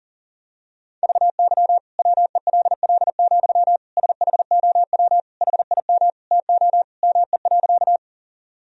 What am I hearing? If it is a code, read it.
Morse code, heard as a keyed tone, about 40 wpm, 700 Hz. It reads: VY WELL, SHOW HIM TO ME.
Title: Sentences from Top 200 Words and Abbreviations 40wpm